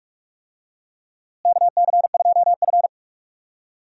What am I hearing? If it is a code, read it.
Morse code, heard as a keyed tone, about 42 wpm, 700 Hz. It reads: KC2F
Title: USA Call Signs - 4 Characters - Repeated 2x - 45wpm